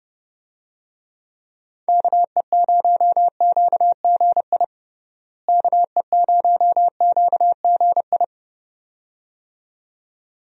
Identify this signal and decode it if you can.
Morse code, heard as a keyed tone, about 30 wpm, 700 Hz. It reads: KE0QGI KE0QGI
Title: USA Call Signs - 6 Characters - Repeated 2x - 30wpm